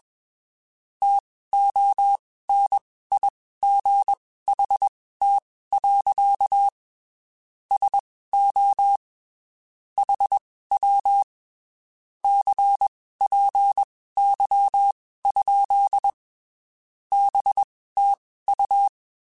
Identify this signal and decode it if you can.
Morse code, heard as a keyed tone, about 21 wpm, spaced as for 15 wpm, 800 Hz. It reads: TONIGHT. SO HW CPY? BTU
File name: (spaced) / QSOs Spaced-mp3s / WA7QOX 1 @20 sp